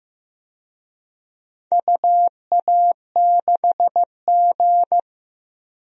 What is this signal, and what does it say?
Morse code, heard as a keyed tone, about 15 wpm, 700 Hz. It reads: UA6G
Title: DXCC-100 Call Signs - 4 Characters - 15wpm